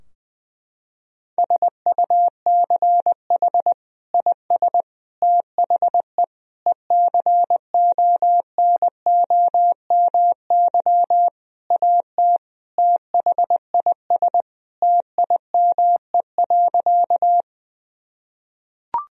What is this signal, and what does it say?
Morse code, heard as a keyed tone, about 20 wpm, 700 Hz. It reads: SUCH IS THE ECONOMY AT THIS TIME.
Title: Sentences from Top 1000 Words - Review Part 3 - 20wpm